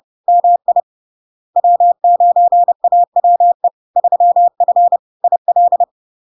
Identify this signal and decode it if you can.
Morse code, heard as a keyed tone, about 30 wpm, 700 Hz. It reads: MI W9AWE 3F IL
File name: LARK-FD-2022-Log-in-CW-30wpm-700-Hz